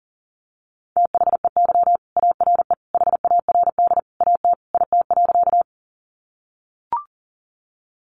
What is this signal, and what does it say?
Morse code, heard as a keyed tone, about 37 wpm, 700 Hz. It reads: THEY ARE HARD AT IT.
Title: Sentences from Top 300 Words 40wpm